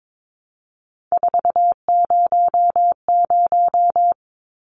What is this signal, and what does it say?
Morse code, heard as a keyed tone, about 22 wpm, 700 Hz. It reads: 400
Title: Numbers - 3-Digit 22wpm